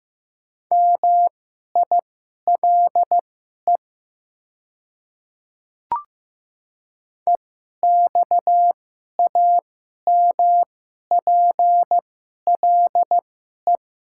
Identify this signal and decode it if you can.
Morse code, heard as a keyed tone, about 15 wpm, spaced as for 11 wpm, 700 Hz. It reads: MILE EXAMPLE
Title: 2x Character Spacing - Top 300 Words - 15wpm